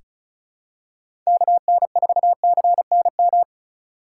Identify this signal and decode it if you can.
Morse code, heard as a keyed tone, about 36 wpm, 700 Hz. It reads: KN4CNM
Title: USA Call Signs - 4x Character Spacing - 35wpm